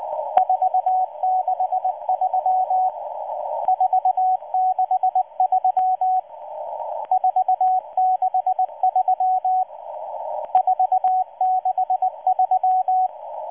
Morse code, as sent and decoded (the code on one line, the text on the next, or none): ....- -.... ...-- / ....- -.... ...-- / ....- -.... ...-- / ....- -.... ...--
463 463 463 463